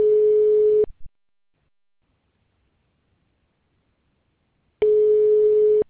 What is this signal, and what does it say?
Ringback tone (Europe, 425 Hz).